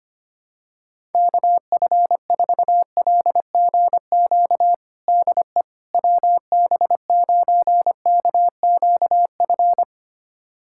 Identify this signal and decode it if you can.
Morse code, heard as a keyed tone, about 25 wpm, 700 Hz. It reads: KF4LGQ DE WB9KQF
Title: USA Callsign DE Callsign - 6 Characters - 25wpm